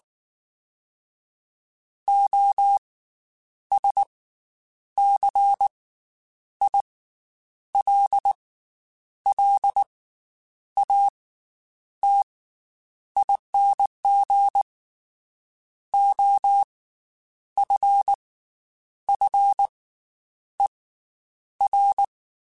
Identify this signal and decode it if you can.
Morse code, heard as a keyed tone, about 19 wpm, spaced as for 8 wpm, 800 Hz. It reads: OSCILLATING OFFER
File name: Special INGs / Special INGs-mp3s / Slow ing @18